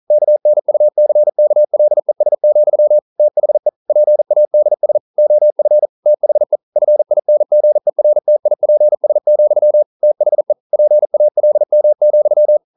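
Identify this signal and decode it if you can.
Morse code, heard as a keyed tone, about 40 wpm, 600 Hz. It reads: KNUCKLES, THE PADS OF THE FINGERTIPS, THE PALM,
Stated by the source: Morse code practice at 41 wpm, from CW med Gnister.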